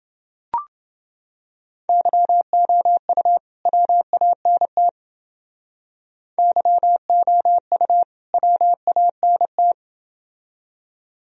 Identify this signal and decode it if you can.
Morse code, heard as a keyed tone, about 29 wpm, 700 Hz. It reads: YOU WANT YOU WANT